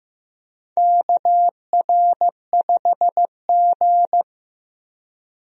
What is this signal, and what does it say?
Morse code, heard as a keyed tone, about 15 wpm, 700 Hz. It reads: KR5G
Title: USA Call Signs - 4 Characters - Repeated 2x - 15wpm